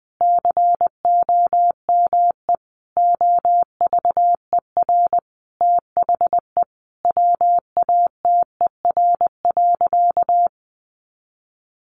Morse code, heard as a keyed tone, about 20 wpm, 700 Hz. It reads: COME OVER THE WATER.
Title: Sentences from Top 300 Words 20wpm